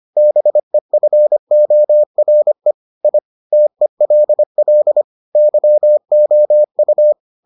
Morse code, heard as a keyed tone, about 25 wpm, 600 Hz. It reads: BEFORE I TELL YOU